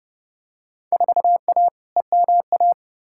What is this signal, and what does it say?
Morse code, heard as a keyed tone, about 30 wpm, 700 Hz. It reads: 4A EMA
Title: Field Day Exchanges - 30wpm